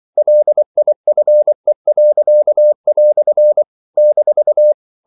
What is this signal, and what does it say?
Morse code, heard as a keyed tone, about 24 wpm, 600 Hz. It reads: LIFE." -
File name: CW / Practice / EnglishQuotations0224